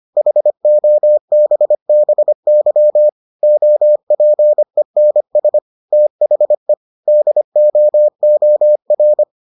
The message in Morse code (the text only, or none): HOBBY OPENS THE DOOR